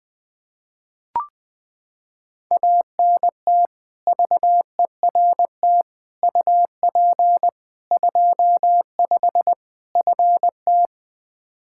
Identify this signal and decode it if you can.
Morse code, heard as a keyed tone, about 20 wpm, 700 Hz. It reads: ANT VERT UP 25 FT
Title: Practice QSO - Exchange 2 - Info - 20wpm